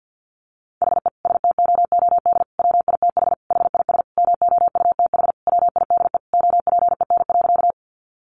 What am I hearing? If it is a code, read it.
Morse code, heard as a keyed tone, about 50 wpm, 700 Hz. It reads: HE STOOD WITH HIS MOUTH WIDE OPEN.
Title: Sentences from Top 1500 Words 50wpm